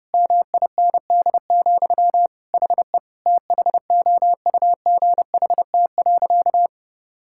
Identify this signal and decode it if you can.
Morse code, heard as a keyed tone, about 30 wpm, 700 Hz. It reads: MIND, HE THOUGHT.
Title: Sentences from Top 1000 Words - Review Part 6 - 30wpm